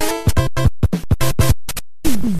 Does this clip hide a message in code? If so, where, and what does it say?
No.